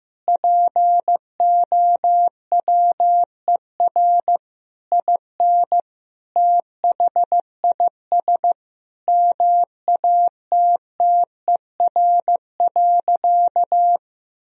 Morse code, heard as a keyed tone, about 16 wpm, 700 Hz. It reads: POWER IN THIS MATTER.
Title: Sentences from Top 1000 Words - Review Part 2 - 15wpm